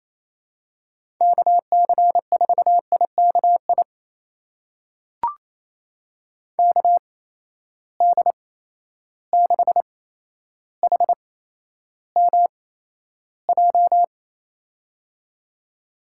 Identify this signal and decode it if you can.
Morse code, heard as a keyed tone, about 28 wpm, spaced as for 12 wpm, 700 Hz. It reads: KC4IKI KD6HMJ
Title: USA Call Signs - 6 Characters - 8x Character Spacing - 28wpm